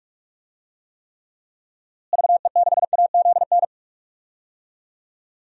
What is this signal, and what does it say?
Morse code, heard as a keyed tone, about 46 wpm, 700 Hz. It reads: VE6AZN
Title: DXCC-100 Call Signs - 6 Characters - Repeated 3x - 45wpm